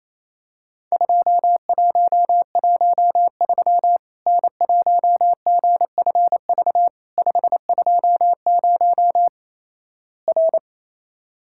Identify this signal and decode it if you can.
Morse code, heard as a keyed tone, about 28 wpm, 700 Hz. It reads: 2113 N1GFV 520 R
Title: Worked All Europe DX Contest - QTC Exchange - 28wpm